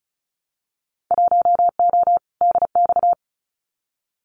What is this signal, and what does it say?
Morse code, heard as a keyed tone, about 36 wpm, 700 Hz. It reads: 1O DX
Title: Winter Field Day Exchanges 35wpm